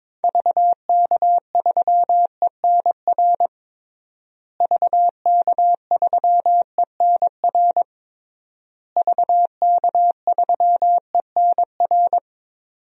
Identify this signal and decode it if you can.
Morse code, heard as a keyed tone, about 22 wpm, 700 Hz. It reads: VK3ENR VK3ENR VK3ENR
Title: DXCC-100 Call Signs - 6 Characters - Repeated 3x - 22wpm